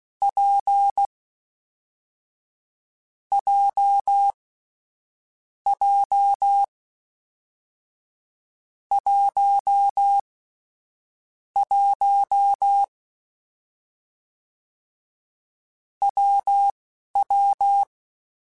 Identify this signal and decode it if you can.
Morse code, heard as a keyed tone, about 16 wpm, spaced as for 11 wpm, 800 Hz. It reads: P J J 1 1 WW
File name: Difficult Combos / Difficult Combos-mp3s / WPJ WPJ1 @10